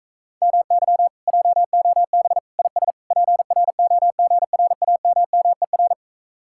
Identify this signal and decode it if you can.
Morse code, heard as a keyed tone, about 37 wpm, 700 Hz. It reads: MY JOB IS PROGRAMMER